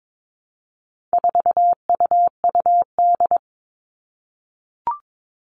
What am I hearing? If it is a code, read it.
Morse code, heard as a keyed tone, about 22 wpm, 700 Hz. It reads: 4UUD
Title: All Cut Numbers - 4-Digit 22wpm